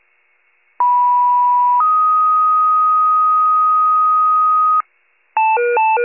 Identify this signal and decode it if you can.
Two-tone page, heard as a one-second tone, then a three-second tone, about 980 then 1250 Hz. It is heard from 0.8 to 4.8 s.
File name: pager / 91 FIRE / wcalert